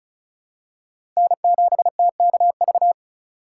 Morse code, heard as a keyed tone, about 34 wpm, 700 Hz. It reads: N7TKV
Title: USA Call Signs - 5 Characters - 4x Character Spacing - 35wpm